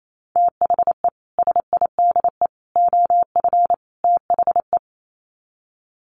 Morse code, heard as a keyed tone, about 28 wpm, 700 Hz. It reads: THE SIDE OF THE
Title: Sets of 4 Words 28wpm